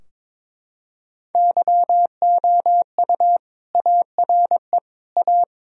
Morse code, heard as a keyed tone, about 21 wpm, 700 Hz. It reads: YOU ARE A
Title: Sentences from Top 1000 Words - Review Part 2 - 22wpm